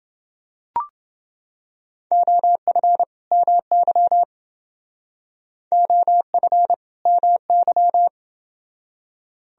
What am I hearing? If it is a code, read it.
Morse code, heard as a keyed tone, about 29 wpm, 700 Hz. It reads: OF MY OF MY